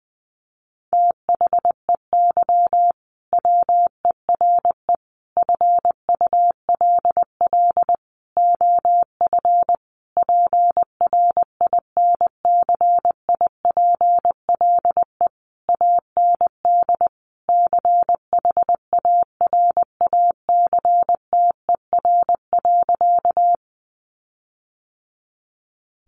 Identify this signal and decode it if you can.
Morse code, heard as a keyed tone, about 20 wpm, 700 Hz. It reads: THEY WERE FULL OF PRINCIPLE AND CHARACTER.
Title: Sentences from Top 1400 Words 20wpm